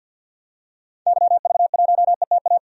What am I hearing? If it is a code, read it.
Morse code, heard as a keyed tone, about 45 wpm, 700 Hz. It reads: YV1ETA
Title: DXCC-100 Call Signs - 6 Characters - 50wpm